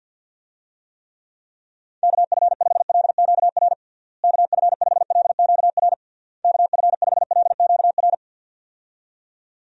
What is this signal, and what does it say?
Morse code, heard as a keyed tone, about 51 wpm, 700 Hz. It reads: KF5LQR KF5LQR KF5LQR
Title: USA Call Signs - 6 Characters - Repeated 3x - 50wpm